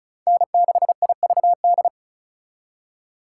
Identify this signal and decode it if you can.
Morse code, heard as a keyed tone, about 45 wpm, 700 Hz. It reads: N6IVD